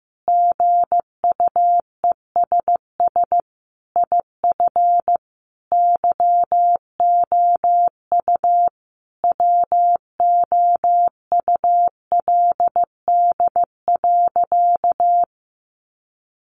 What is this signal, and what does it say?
Morse code, heard as a keyed tone, about 14 wpm, 700 Hz. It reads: GUESS IF YOU WOULD.
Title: Sentences from Top 1500 Words 15wpm